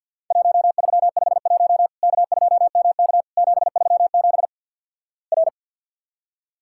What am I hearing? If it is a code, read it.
Morse code, heard as a keyed tone, about 50 wpm, 700 Hz. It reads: 1351 K2MK 737 R
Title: Worked All Europe DX Contest - QTC Exchange with Repeats - 50wpm